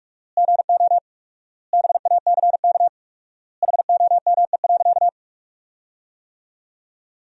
Morse code, heard as a keyed tone, about 46 wpm, 700 Hz. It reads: GO BACK HOME.
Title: 4x Word-Spacing - Sentences from Top 500 Words - 45wpm